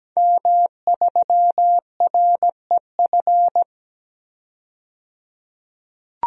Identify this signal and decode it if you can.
Morse code, heard as a keyed tone, about 17 wpm, 700 Hz. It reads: M3REF
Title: DXCC-100 Call Signs - 5 Characters - 17wpm